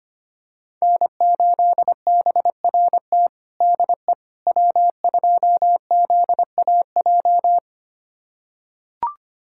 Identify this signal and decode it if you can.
Morse code, heard as a keyed tone, about 25 wpm, 700 Hz. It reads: N8BRT DE W2ZAJ
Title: USA Callsign DE Callsign - 5 Characters - 25wpm